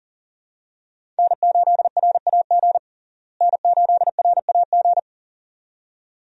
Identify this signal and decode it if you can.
Morse code, heard as a keyed tone, about 40 wpm, 700 Hz. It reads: N8RAG N8RAG
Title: USA Call Signs - 5 Characters - Repeated 2x - 40wpm